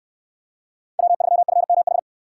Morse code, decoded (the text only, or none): W3FMH